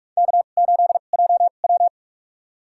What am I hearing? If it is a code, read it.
Morse code, heard as a keyed tone, about 44 wpm, spaced as for 32 wpm, 700 Hz. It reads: K8JW
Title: USA Call Signs - 4 Characters - 2x Character Spacing - 45wpm